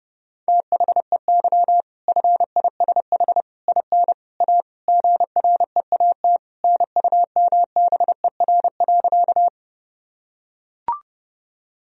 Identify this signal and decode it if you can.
Morse code, heard as a keyed tone, about 29 wpm, 700 Hz. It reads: THEY FISH IN A GREAT NUMBER.
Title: Sentences from Top 1000 Words - Review Part 1 - 30wpm